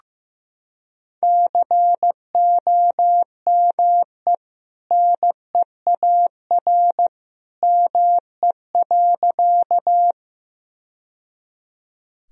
Morse code, heard as a keyed tone, about 15 wpm, 700 Hz. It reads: COME NEAR ME.
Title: Sentences from Top 1000 Words - Review Part 1 - 15wpm